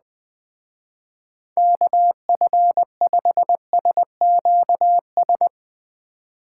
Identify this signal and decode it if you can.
Morse code, heard as a keyed tone, about 20 wpm, 700 Hz. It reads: KF5SQS
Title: Speed Racing - USA 2x3 Call Signs 17wpm